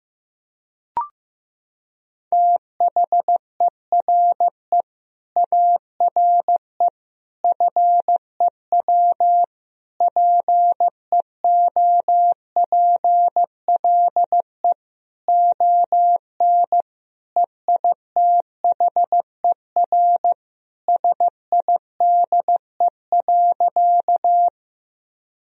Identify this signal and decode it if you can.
Morse code, heard as a keyed tone, about 16 wpm, 700 Hz. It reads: THERE ARE FEW PEOPLE ON EITHER SIDE.
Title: Sentences from Top 1000 Words - Review Part 4 - 15wpm